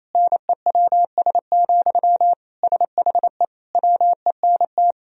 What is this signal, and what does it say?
Morse code, heard as a keyed tone, about 28 wpm, 700 Hz. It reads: NEWS, SHE WENT
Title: Sentences from Top 1000 Words - Review Part 3 - 28wpm